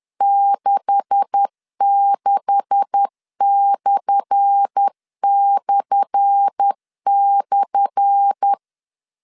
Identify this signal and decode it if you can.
Morse code, heard as a keyed tone, about 10 wpm, 800 Hz. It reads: 66///
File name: LEZIONI / 10 SIGNS / 10A SLASH 10wpm 2808